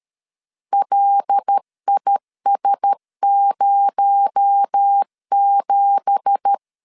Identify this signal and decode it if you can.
Morse code, heard as a keyed tone, about 13 wpm, 800 Hz. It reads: LIS07
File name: CW / DET / 1 9 1500C S A 12wpm 3800